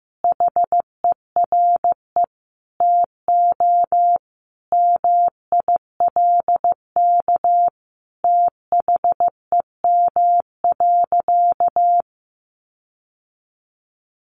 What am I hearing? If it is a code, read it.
Morse code, heard as a keyed tone, about 15 wpm, 700 Hz. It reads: HERE TO MILK THEM.
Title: Sentences from Top 1500 Words 15wpm